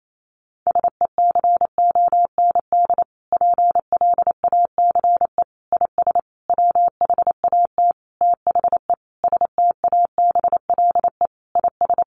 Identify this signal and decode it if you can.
Morse code, heard as a keyed tone, about 28 wpm, 700 Hz. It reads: SECOND PLACE IS WHAT THE STABLE IS
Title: Sentences from Top 1500 Words 28wpm